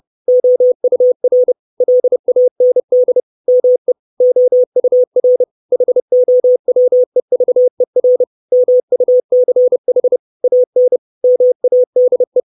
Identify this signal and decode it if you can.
Morse code, heard as a keyed tone, about 30 wpm, 500 Hz. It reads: OUR LAND ME OUR HOWEVER MUCH AN MADE
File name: 632 Words 30WPM